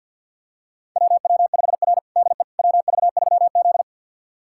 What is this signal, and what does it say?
Morse code, heard as a keyed tone, about 50 wpm, 700 Hz. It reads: WW5R DE WV3Z
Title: USA Callsign DE Callsign - 4 Characters - 50wpm